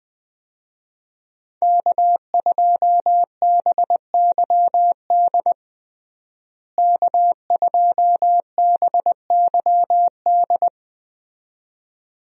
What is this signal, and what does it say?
Morse code, heard as a keyed tone, about 20 wpm, 700 Hz. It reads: K2BYD K2BYD
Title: USA Call Signs - 5 Characters - Repeated 2x - 20wpm